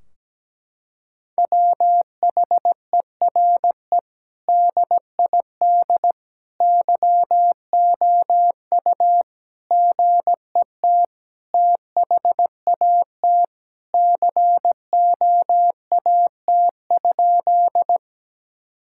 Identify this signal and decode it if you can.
Morse code, heard as a keyed tone, about 17 wpm, 700 Hz. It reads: WHERE DID YOU GET THAT COAT?
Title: Sentences from Top 1600 Words 17wpm